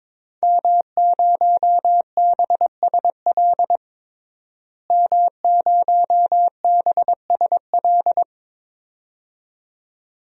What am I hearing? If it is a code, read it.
Morse code, heard as a keyed tone, about 23 wpm, 700 Hz. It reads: M0BSL M0BSL
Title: DXCC-100 Call Signs - 5 Characters - Repeated 2x - 22wpm